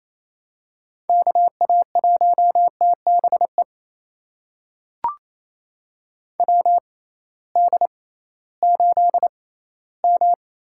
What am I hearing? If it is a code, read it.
Morse code, heard as a keyed tone, about 28 wpm, spaced as for 16 wpm, 700 Hz. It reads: KA1TBE WD8M
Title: USA Call Signs - 6 Characters - 6x Character Spacing - 28wpm